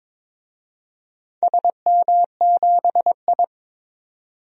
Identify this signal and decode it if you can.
Morse code, heard as a keyed tone, about 22 wpm, 700 Hz. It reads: SM7I
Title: DXCC-100 Call Signs - 4 Characters - 22wpm